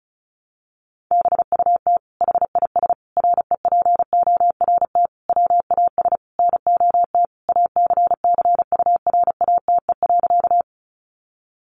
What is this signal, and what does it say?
Morse code, heard as a keyed tone, about 34 wpm, 700 Hz. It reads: BUT HIS REPORT WAS NOT ACCURATE.
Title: Sentences from Top 1200 Words 35wpm